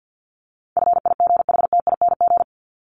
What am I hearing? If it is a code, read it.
Morse code, heard as a keyed tone, about 50 wpm, 700 Hz. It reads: FIGHTING
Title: ING Suffix - Top 100 Words with Base Words 50wpm